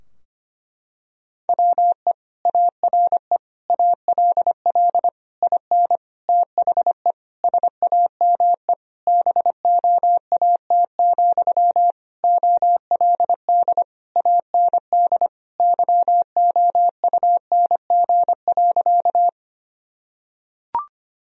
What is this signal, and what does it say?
Morse code, heard as a keyed tone, about 25 wpm, 700 Hz. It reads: WE ARE ALL IN THE SAME BOAT, OLD AND YOUNG.
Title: Sentences from Top 1000 Words - Review Part 3 - 25wpm